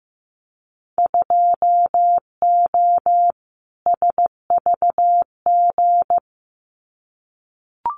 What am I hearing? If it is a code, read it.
Morse code, heard as a keyed tone, about 15 wpm, 700 Hz. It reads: 2O SVG
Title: Winter Field Day Exchanges 15wpm